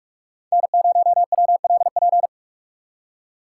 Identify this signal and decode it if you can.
Morse code, heard as a keyed tone, about 41 wpm, 700 Hz. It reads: N0WLP